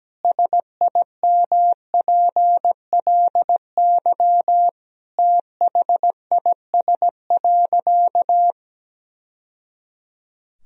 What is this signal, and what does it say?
Morse code, heard as a keyed tone, about 17 wpm, 700 Hz. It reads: SIMPLY THIS.
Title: Sentences from Top 1000 Words - Review Part 4 - 17wpm